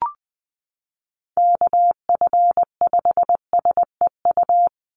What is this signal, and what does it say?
Morse code, heard as a keyed tone, about 19 wpm, 700 Hz. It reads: KF5SEU
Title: USA Call Signs 20wpm